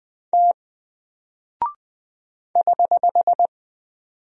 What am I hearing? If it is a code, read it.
Morse code, heard as a keyed tone, about 20 wpm, 700 Hz. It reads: T <HH>
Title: Single Character (Letter:Number:Punctuation:Prosign) 20wpm